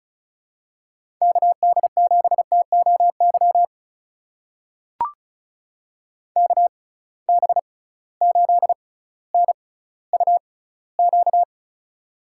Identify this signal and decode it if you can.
Morse code, heard as a keyed tone, about 36 wpm, spaced as for 18 wpm, 700 Hz. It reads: KD7TOY KB8NUQ
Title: USA Call Signs - 6 Characters - 6x Character Spacing - 35wpm